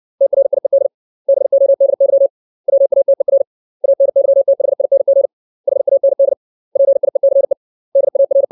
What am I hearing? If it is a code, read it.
Morse code, heard as a keyed tone, about 60 wpm, 550 Hz. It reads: TRIED BODY WATER ANYTHING HAND PIECE DAR
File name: 632 words / 632 Words 60WPM